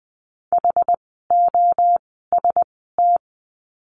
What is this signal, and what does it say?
Morse code, heard as a keyed tone, about 20 wpm, 700 Hz. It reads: H O S T
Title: 061 Lesson 11f - Letter h - Words 20wpm